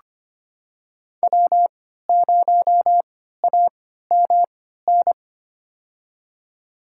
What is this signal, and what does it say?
Morse code, heard as a keyed tone, about 25 wpm, spaced as for 14 wpm, 700 Hz. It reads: W0AMN
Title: USA Call Signs - 5 Characters - 3x Character Spacing - 25wpm